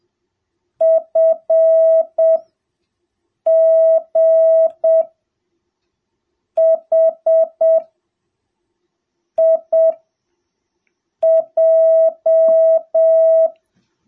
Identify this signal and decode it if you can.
Morse code, heard as a keyed tone, about 7 wpm, spaced as for 4 wpm, 660 Hz. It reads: FGHIJ